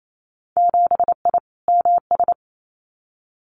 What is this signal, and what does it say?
Morse code, heard as a keyed tone, about 28 wpm, 700 Hz. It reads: 7I MS
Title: Winter Field Day Exchanges Encore 28wpm